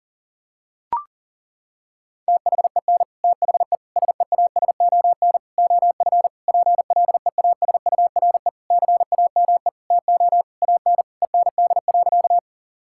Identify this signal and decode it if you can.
Morse code, heard as a keyed tone, about 40 wpm, 700 Hz. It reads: THEN THE SEASON OF PLEASURE CAME TO AN END.
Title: Sentences from Top 1600 Words 40wpm